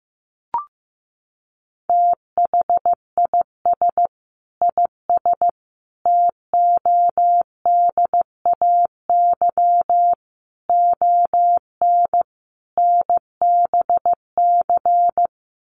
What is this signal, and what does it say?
Morse code, heard as a keyed tone, about 15 wpm, 700 Hz. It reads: THIS IS TODAY ON NBC
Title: Sets of 5 Words Encore 15wpm